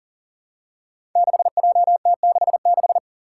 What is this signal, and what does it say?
Morse code, heard as a keyed tone, about 42 wpm, 700 Hz. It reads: 6JT66
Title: Five Challenging Letter-Number Combinations 40wpm